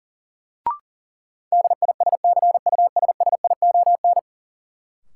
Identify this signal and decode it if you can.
Morse code, heard as a keyed tone, about 40 wpm, 700 Hz. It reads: DISCUSSION
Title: Sentences from Top 1100 Words - New Word Drill - 40wpm